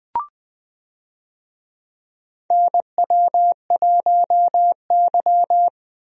Morse code, heard as a keyed tone, about 21 wpm, 700 Hz. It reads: NW1Y
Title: USA Call Signs - 4 Characters - Repeated 2x - 20wpm